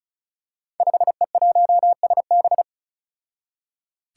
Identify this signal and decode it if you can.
Morse code, heard as a keyed tone, about 36 wpm, 700 Hz. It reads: 5E1SB
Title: Five Challenging Letter-Number Combinations 35wpm